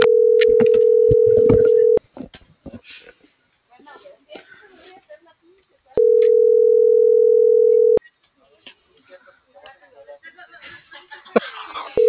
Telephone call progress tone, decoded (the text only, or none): ringback tone